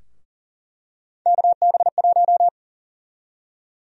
Morse code, heard as a keyed tone, about 40 wpm, 700 Hz. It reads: KB1
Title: USA Call Sign Prefixes - Repeated 2x - 40wpm